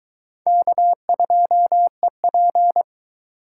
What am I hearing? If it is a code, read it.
Morse code, heard as a keyed tone, about 22 wpm, 700 Hz. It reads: K2EP